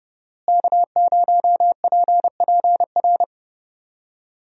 Morse code, heard as a keyed tone, about 30 wpm, 700 Hz. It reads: K0PPR
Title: USA Call Signs - 5 Characters - Repeated 3x - 30wpm